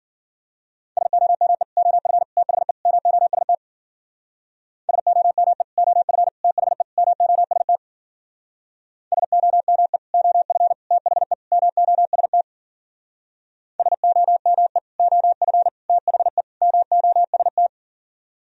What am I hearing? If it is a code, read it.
Morse code, heard as a keyed tone, about 51 wpm, 700 Hz. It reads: SOME OF THE MOST SOME OF THE MOST SOME OF THE MOST SOME OF THE MOST